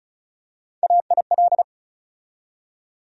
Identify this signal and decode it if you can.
Morse code, heard as a keyed tone, about 36 wpm, 700 Hz. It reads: AIL
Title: USA Call Sign Postfixes - 3 Letters 35wpm